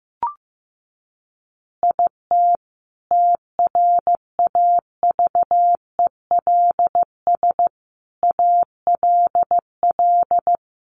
Morse code, heard as a keyed tone, about 14 wpm, 700 Hz. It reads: IT TRAVELS ALL
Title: Additional Easy Phrases 15wpm